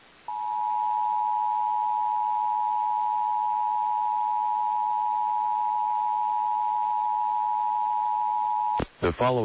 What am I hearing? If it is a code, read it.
Emergency Alert System (SAME) tones (the 853 and 960 Hz attention tone), with speech. It is heard from 0.2 to 8.8 s.
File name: EAS Logs / KMNR